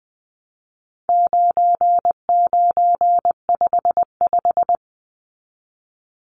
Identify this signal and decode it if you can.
Morse code, heard as a keyed tone, about 20 wpm, 700 Hz. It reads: 9955
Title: Numbers - 4-Digit 20wpm